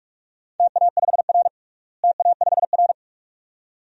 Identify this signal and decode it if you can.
Morse code, heard as a keyed tone, about 45 wpm, 700 Hz. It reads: TA5R TA5R
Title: DXCC-100 Call Signs - 4 Characters - Repeated 2x - 45wpm